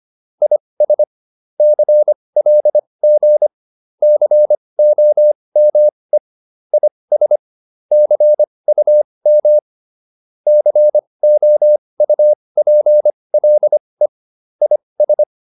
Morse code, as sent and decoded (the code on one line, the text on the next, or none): .. ... / -.-. .-.. --. / -.-. --- -- . / .. ... / -.-. ..- -- / -.-. --- ..- .--. .-.. / .. ...
IS CLG COME IS CUM COUPLE IS